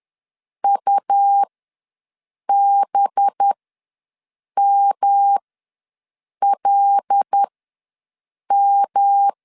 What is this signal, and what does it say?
Morse code, heard as a keyed tone, about 10 wpm, spaced as for 6 wpm, 800 Hz. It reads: UBMLM
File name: LEZIONI / 5 LBVC / 5 LB ws 10wpm 3616